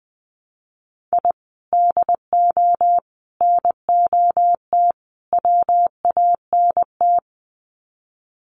Morse code, heard as a keyed tone, about 20 wpm, 700 Hz. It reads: I DO NOT WANT
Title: Sets of 4 Words 20wpm